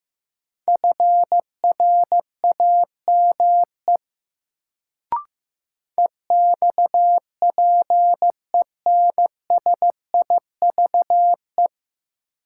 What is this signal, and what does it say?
Morse code, heard as a keyed tone, about 15 wpm, 700 Hz. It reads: FRAME EXPENSIVE
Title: Sentences from Top 1000 Words - Review Part 6 - New Word Drill - 15wpm